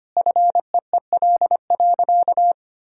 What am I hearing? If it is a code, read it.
Morse code, heard as a keyed tone, about 25 wpm, 700 Hz. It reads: FEEL.
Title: Sentences from Top 1000 Words - Review Part 3 - 25wpm